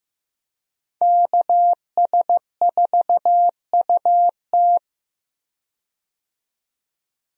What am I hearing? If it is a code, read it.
Morse code, heard as a keyed tone, about 15 wpm, 700 Hz. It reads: KS4UT
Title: USA Call Signs - 5 Characters - 15wpm